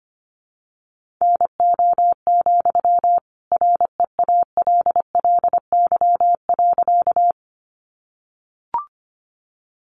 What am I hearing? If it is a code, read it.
Morse code, heard as a keyed tone, about 25 wpm, 700 Hz. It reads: NO, REALLY.
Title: Sentences from Top 600 Words 25wpm